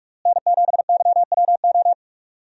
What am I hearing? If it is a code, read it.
Morse code, heard as a keyed tone, about 42 wpm, 700 Hz. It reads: N7YWO
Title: USA Call Signs - 5 Characters - Repeated 3x - 45wpm